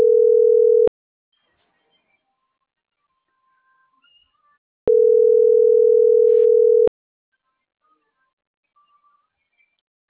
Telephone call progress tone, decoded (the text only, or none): ringback tone